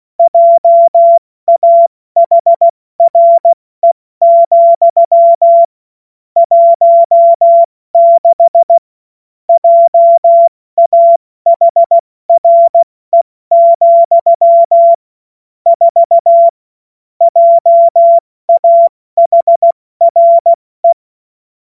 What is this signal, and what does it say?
Morse code, heard as a keyed tone, about 16 wpm, spaced as for 14 wpm, 680 Hz. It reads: JAHRE, 16 JAHRE, 4 JAHRE
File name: Witz-08-Juli-2025-70-F80